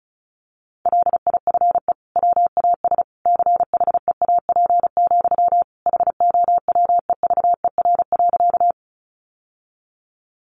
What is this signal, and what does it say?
Morse code, heard as a keyed tone, about 34 wpm, 700 Hz. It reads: LIFE WAS CHEAP, HOWEVER.
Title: Sentences from Top 1200 Words 35wpm